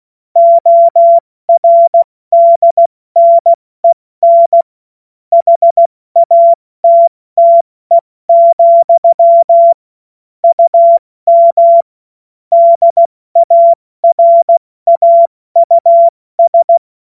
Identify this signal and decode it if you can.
Morse code, heard as a keyed tone, about 16 wpm, 680 Hz. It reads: ORDNEN HATTE, UM DARAUS
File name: Morsekurs / Witze / Witz 21.November 2023 70 F80